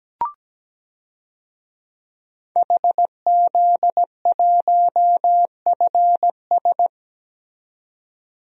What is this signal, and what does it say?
Morse code, heard as a keyed tone, about 17 wpm, 700 Hz. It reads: HZ1FS
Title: DXCC-100 Call Signs - 5 Characters - 17wpm